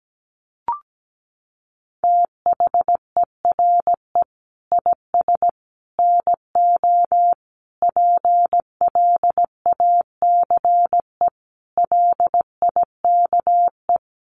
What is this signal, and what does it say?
Morse code, heard as a keyed tone, about 17 wpm, 700 Hz. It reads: THERE IS NO PLACE LIKE
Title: Sentences from Top 200 Words 17wpm